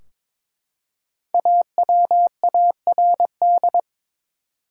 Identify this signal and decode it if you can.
Morse code, heard as a keyed tone, about 21 wpm, 700 Hz. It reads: AWARD
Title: Five Letter Words - 22wpm